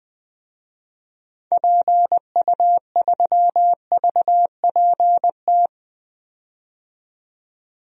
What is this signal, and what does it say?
Morse code, heard as a keyed tone, about 21 wpm, 700 Hz. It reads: PU3VPT
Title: DXCC-100 Call Signs - 6 Characters - 20wpm